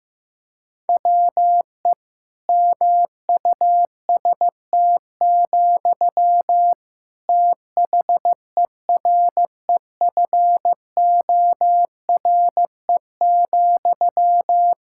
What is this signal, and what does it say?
Morse code, heard as a keyed tone, about 16 wpm, 700 Hz. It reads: WE MUST, THEREFORE,
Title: Sentences from Top 1600 Words 15wpm